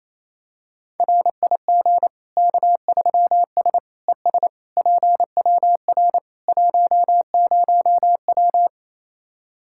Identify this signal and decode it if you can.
Morse code, heard as a keyed tone, about 28 wpm, 700 Hz. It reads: RIG K3S ES PWR 10W
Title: Practice QSO - Exchange 2 - Info - 28wpm